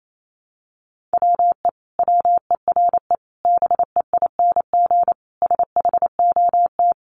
Morse code, heard as a keyed tone, about 28 wpm, 700 Hz. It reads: WE WERE BEING SHOT
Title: Sentences from Top 1100 Words 28wpm